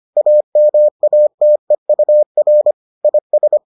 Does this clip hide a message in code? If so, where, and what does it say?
Yes: Morse code from 0.2 to 3.6 s, heard as a keyed tone, about 25 wpm, 600 Hz. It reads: AMATEUR IS